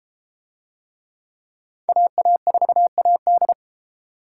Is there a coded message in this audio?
Yes: Morse code from 1.9 to 3.5 s, heard as a keyed tone, about 30 wpm, 700 Hz. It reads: AA4AD